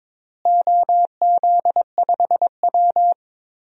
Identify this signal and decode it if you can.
Morse code, heard as a keyed tone, about 22 wpm, 700 Hz. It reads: OZ5W
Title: DXCC-100 Call Signs - 4 Characters - Repeated 2x - 22wpm